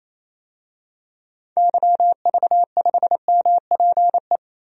Morse code, heard as a keyed tone, about 28 wpm, 700 Hz. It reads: YV5MPE
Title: DXCC-100 Call Signs - 6 Characters - Repeated 3x - 28wpm